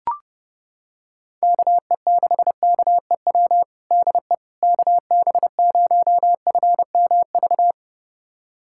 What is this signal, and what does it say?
Morse code, heard as a keyed tone, about 30 wpm, 700 Hz. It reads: KE6KEW DE KB0FMV
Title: USA Callsign DE Callsign 30wpm